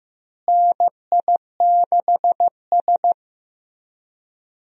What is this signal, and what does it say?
Morse code, heard as a keyed tone, about 15 wpm, 700 Hz. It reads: NI6S
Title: USA Call Signs - 4 Characters - Repeated 3x - 15wpm